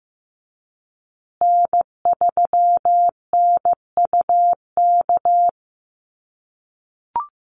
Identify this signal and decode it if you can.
Morse code, heard as a keyed tone, about 15 wpm, 700 Hz. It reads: N3NUK
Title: USA Call Signs Repeated 2x 15wpm